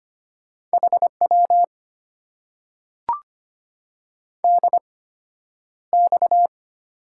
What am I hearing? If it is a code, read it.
Morse code, heard as a keyed tone, about 25 wpm, spaced as for 11 wpm, 690 Hz. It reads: HW DX
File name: Top 90 QSO Elements - 8x Character Spacing - 25wpm